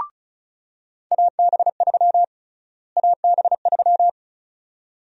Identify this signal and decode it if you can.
Morse code, heard as a keyed tone, about 34 wpm, 700 Hz. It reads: AB3 AB3
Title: USA Call Sign Prefixes - Repeated 2x - 35wpm